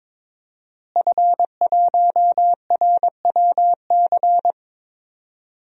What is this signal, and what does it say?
Morse code, heard as a keyed tone, about 21 wpm, 700 Hz. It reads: F1RWC
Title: DXCC-100 Call Signs - 5 Characters - Repeated 3x - 22wpm